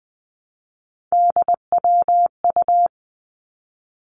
Morse code, heard as a keyed tone, about 19 wpm, 700 Hz. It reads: DWU
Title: Three Challenging Letter-Number Combinations 20wpm